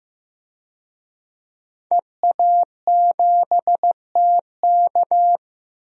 Morse code, heard as a keyed tone, about 15 wpm, 700 Hz. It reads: EA7TK
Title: DXCC-100 Call Signs - 5 Characters - 15wpm